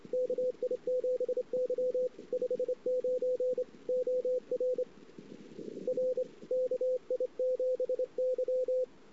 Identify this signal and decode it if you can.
Morse code, heard as a keyed tone, about 27 wpm, 500 Hz, with radio noise. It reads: KI7Y 59 OR R KI7Y
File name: CW / 2008-ss-cw-k5tr-op-n5rz -19:00